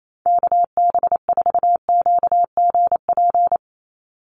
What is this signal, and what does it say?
Morse code, heard as a keyed tone, about 28 wpm, 700 Hz. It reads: KB4QGP
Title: USA Call Signs 28wpm